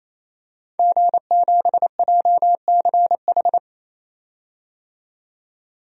Morse code, heard as a keyed tone, about 28 wpm, 700 Hz. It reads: G7JCH
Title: DXCC-100 Call Signs - 5 Characters - 28wpm